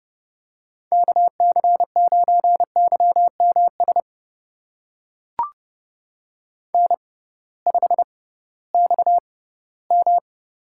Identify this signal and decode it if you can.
Morse code, heard as a keyed tone, about 30 wpm, spaced as for 17 wpm, 700 Hz. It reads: KC9YMS N5XM
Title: USA Call Signs - 6x Character Spacing - 30wpm